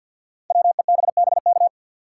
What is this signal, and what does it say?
Morse code, heard as a keyed tone, about 52 wpm, 700 Hz. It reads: WEBBK
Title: Code Groups - 5 Letter:Numbers - 50wpm